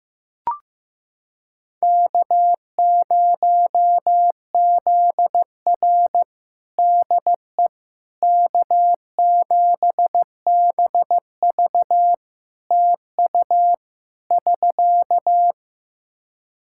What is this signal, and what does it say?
Morse code, heard as a keyed tone, about 15 wpm, 700 Hz. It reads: K0ZR DE K7BV TU <SK>
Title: Practice QSO - Exchange 4 - Ending - 15wpm